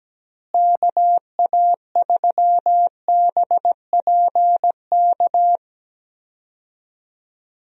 Morse code, heard as a keyed tone, about 17 wpm, 700 Hz. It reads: KA3BPK
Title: USA Call Signs - 6 Characters - Repeated 3x - 17wpm